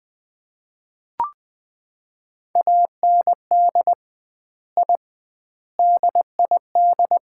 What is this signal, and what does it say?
Morse code, heard as a keyed tone, about 21 wpm, 700 Hz. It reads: AND I DID
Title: 2x Word-Spacing - Sets of 5 Words - 20wpm